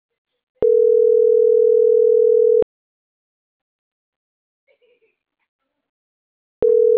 Ringback tone (North America), 440 + 480 Hz.